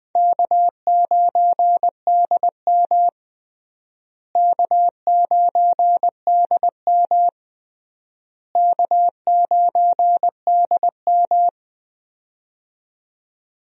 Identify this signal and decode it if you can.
Morse code, heard as a keyed tone, about 21 wpm, 700 Hz. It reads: K9DM K9DM K9DM
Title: USA Call Signs - 4 Characters - Repeated 3x - 20wpm